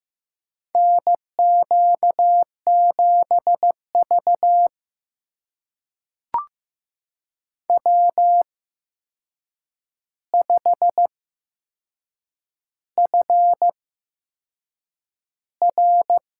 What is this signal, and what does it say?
Morse code, heard as a keyed tone, about 15 wpm, spaced as for 6 wpm, 700 Hz. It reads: NQ7V W5FR
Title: USA Call Signs - 4 Characters - 8x Character Spacing - 15wpm